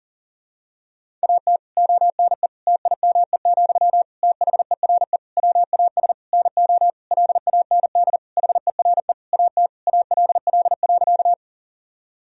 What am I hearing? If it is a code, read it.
Morse code, heard as a keyed tone, about 40 wpm, 700 Hz. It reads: AT ONE TIME, THERE WAS NO LAND HERE AT ALL.
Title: Sentences from Top 1000 Words - Review Part 1 - 40wpm